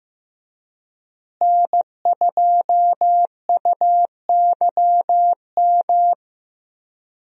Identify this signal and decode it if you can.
Morse code, heard as a keyed tone, about 15 wpm, 700 Hz. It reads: N2UYM
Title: USA Call Signs - 5 Characters - Repeated 2x - 15wpm